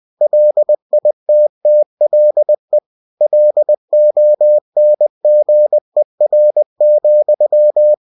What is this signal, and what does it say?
Morse code, heard as a keyed tone, about 20 wpm, 600 Hz. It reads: LITTLE LONGER,
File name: cw-books / winnie-the-pooh / winnie-20wpm-page-0034